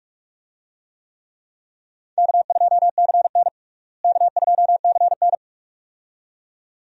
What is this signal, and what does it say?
Morse code, heard as a keyed tone, about 42 wpm, 700 Hz. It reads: K2CN K2CN
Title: USA Call Signs - 4 Characters - Repeated 2x - 45wpm